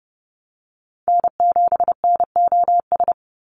Morse code, heard as a keyed tone, about 30 wpm, 700 Hz. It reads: N7NOS
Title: USA Call Signs 30wpm